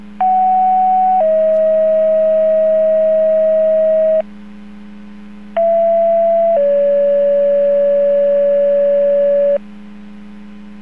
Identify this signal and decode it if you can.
Two-tone page, heard as a one-second tone, then a three-second tone, about 750 then 640 Hz, about 670 then 570 Hz. It is heard from 0.2 to 9.6 s.